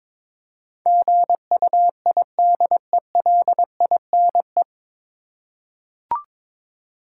Morse code, heard as a keyed tone, about 22 wpm, 700 Hz. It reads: GUIDELINE
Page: Morse Code Ninja practice 22wpm